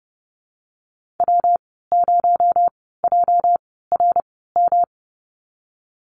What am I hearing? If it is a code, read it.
Morse code, heard as a keyed tone, about 31 wpm, spaced as for 17 wpm, 700 Hz. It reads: W0JRM
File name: Lesson 37g - Letter x - Call Signs 30wpm